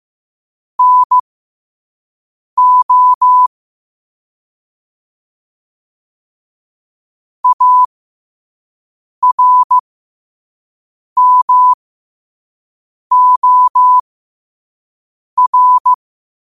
Morse code, heard as a keyed tone, about 15 wpm, spaced as for 5 wpm, 1000 Hz. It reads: NO ARMOR
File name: qotdfiles / 5wpm